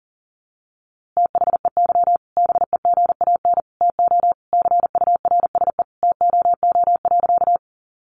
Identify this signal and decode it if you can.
Morse code, heard as a keyed tone, about 40 wpm, 700 Hz. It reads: THEY BEGAN TO CURSE TOO.
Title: A Sound Of Thunder 40wpm